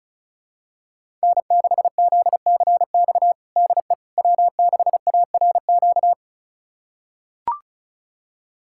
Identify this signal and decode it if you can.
Morse code, heard as a keyed tone, about 36 wpm, 700 Hz. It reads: N6ZCX DE W6ARQ
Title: USA Callsign DE Callsign - 5 Characters - 35wpm